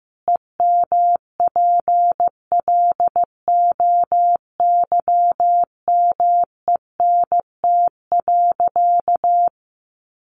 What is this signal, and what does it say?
Morse code, heard as a keyed tone, about 15 wpm, 700 Hz. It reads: EMPLOYMENT.
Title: Sentences from Top 1500 Words 15wpm